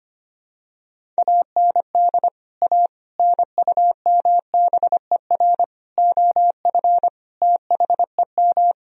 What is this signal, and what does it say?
Morse code, heard as a keyed tone, about 25 wpm, 700 Hz. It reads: AND A NUMBER OF THEM
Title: Sentences from Top 1000 Words - Review Part 1 - 25wpm